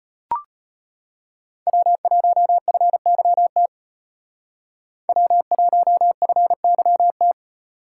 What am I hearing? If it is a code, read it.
Morse code, heard as a keyed tone, about 34 wpm, 700 Hz. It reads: W1FYT W1FYT